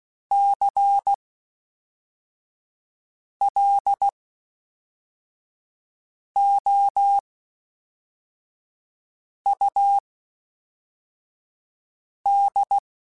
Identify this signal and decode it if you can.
Morse code, heard as a keyed tone, about 16 wpm, spaced as for 4 wpm, 800 Hz. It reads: CLOUD
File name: Misc / Misc-mp3s / WordPairs sp @5